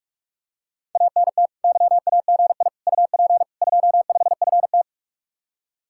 Morse code, heard as a keyed tone, about 47 wpm, 700 Hz. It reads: ANT YAGI UP 25FT